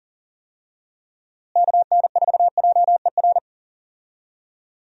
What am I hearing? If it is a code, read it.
Morse code, heard as a keyed tone, about 40 wpm, 700 Hz. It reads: KN4JER
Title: USA Call Signs - 6 Characters - Repeated 2x - 40wpm